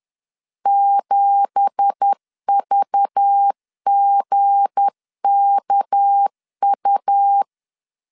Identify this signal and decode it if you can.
Morse code, heard as a keyed tone, about 10 wpm, 800 Hz. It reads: 7VGKU
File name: CW / DET / 1 9 1500C S A 10wpm 4511